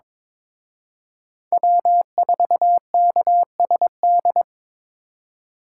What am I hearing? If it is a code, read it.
Morse code, heard as a keyed tone, about 26 wpm, 700 Hz. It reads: W4KSD
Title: Speed Racing - USA 1x3 Call Signs 22wpm